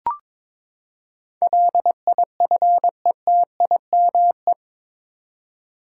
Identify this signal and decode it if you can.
Morse code, heard as a keyed tone, about 22 wpm, 700 Hz. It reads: LIFETIME
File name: practice / 22wpm / Eight Letter Words - 22wpm